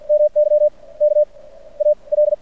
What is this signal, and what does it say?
Morse code, heard as a keyed tone, about 50 wpm, spaced as for 24 wpm, 600 Hz. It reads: MY KAL
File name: archived / CRSnet